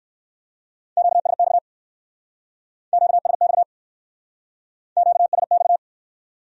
Morse code, heard as a keyed tone, about 60 wpm, 700 Hz. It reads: QSX QSX QSX